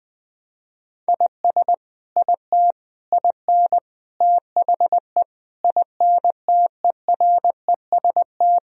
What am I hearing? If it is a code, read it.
Morse code, heard as a keyed tone, about 20 wpm, 700 Hz. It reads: IS IT IN THE INTEREST
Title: Sentences from Top 1000 Words - Review Part 4 - 20wpm